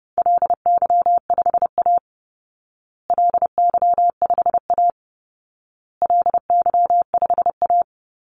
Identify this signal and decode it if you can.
Morse code, heard as a keyed tone, about 30 wpm, 700 Hz. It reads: LY5A LY5A LY5A
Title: Call Signs Repeated 2x - 30wpm